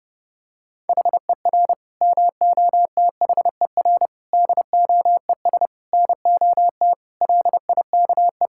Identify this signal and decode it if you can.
Morse code, heard as a keyed tone, about 30 wpm, 700 Hz. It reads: HER MOTHER DOES NOT LIKE
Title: Sentences from Top 1000 Words - Review Part 5 - 30wpm